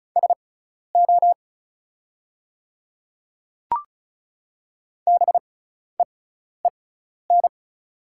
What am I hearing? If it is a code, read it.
Morse code, heard as a keyed tone, about 36 wpm, spaced as for 12 wpm, 700 Hz. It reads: SO BEEN
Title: 6x Character Spacing - Top 100 Words - 35wpm